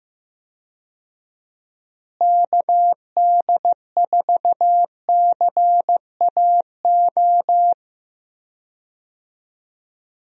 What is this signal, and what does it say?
Morse code, heard as a keyed tone, about 15 wpm, 700 Hz. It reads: KD4CAO
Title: USA Call Signs - 6 Characters - Repeated 2x - 15wpm